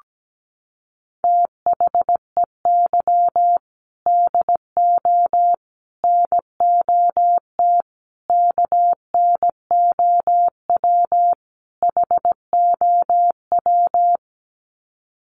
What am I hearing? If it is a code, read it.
Morse code, heard as a keyed tone, about 17 wpm, 700 Hz. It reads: THEY DO NOT KNOW HOW
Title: Sets of 5 Words Encore2 17wpm